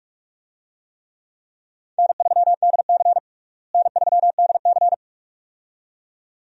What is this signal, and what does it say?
Morse code, heard as a keyed tone, about 42 wpm, 700 Hz. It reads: N3DC N3DC
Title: USA Call Signs - 4 Characters - Repeated 2x - 45wpm